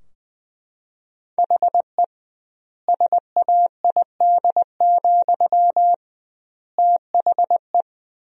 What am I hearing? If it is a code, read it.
Morse code, heard as a keyed tone, about 20 wpm, 700 Hz. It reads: HE SAID, THE
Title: 2x Word-Spacing - Sentences from Top 400 Words - 20wpm